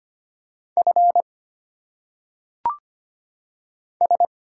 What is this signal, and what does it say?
Morse code, heard as a keyed tone, about 25 wpm, 700 Hz. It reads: F S
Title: Single Letters - Phonetic Pronunciation 25wpm